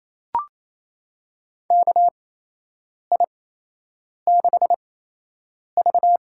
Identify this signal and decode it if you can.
Morse code, heard as a keyed tone, about 28 wpm, spaced as for 8 wpm, 700 Hz. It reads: KI6V
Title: USA Call Signs - 6 Characters - 8x Character Spacing - 28wpm